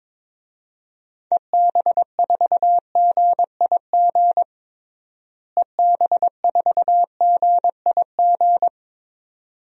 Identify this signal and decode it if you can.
Morse code, heard as a keyed tone, about 22 wpm, 700 Hz. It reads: EB4GIG EB4GIG
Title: DXCC-100 Call Signs - 6 Characters - Repeated 2x - 22wpm